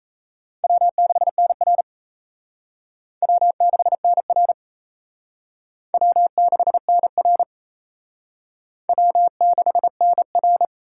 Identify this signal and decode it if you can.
Morse code, heard as a keyed tone, about 34 wpm, 700 Hz. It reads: W6NR W6NR W6NR W6NR
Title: Speed Racing - USA 1x2 Call Signs 28wpm